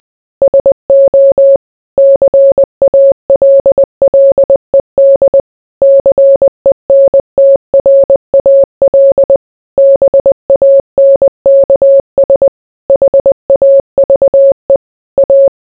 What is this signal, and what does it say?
Morse code, heard as a keyed tone, about 20 wpm, 550 Hz. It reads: SO CALLED CENTRAL BANKS HAVE A